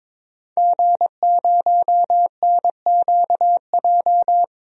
Morse code, heard as a keyed tone, about 22 wpm, 700 Hz. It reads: G0NQJ
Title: DXCC-100 Call Signs - 5 Characters - 22wpm